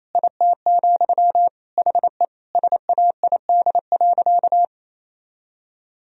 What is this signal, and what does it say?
Morse code, heard as a keyed tone, about 28 wpm, 700 Hz. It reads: IT, HE SAID.
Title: Sentences from Top 1000 Words - Review Part 1 - 28wpm